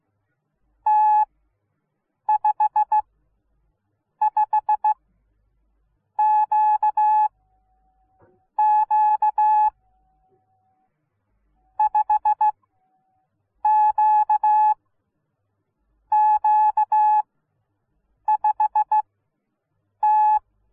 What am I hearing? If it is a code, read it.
Morse code, heard as a keyed tone, about 16 wpm, spaced as for 6 wpm, 850 Hz. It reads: T55QQ 5QQ5T